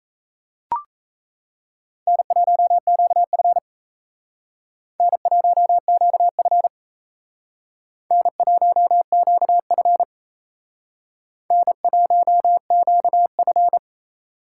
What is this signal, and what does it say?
Morse code, heard as a keyed tone, about 35 wpm, 700 Hz. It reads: N1QF N1QF N1QF N1QF